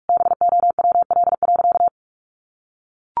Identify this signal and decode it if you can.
Morse code, heard as a keyed tone, about 44 wpm, 700 Hz. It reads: BOWL.
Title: Sentences from Top 1200 Words 45wpm